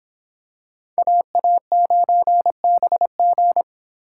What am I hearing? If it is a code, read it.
Morse code, heard as a keyed tone, about 25 wpm, 700 Hz. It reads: AA9BG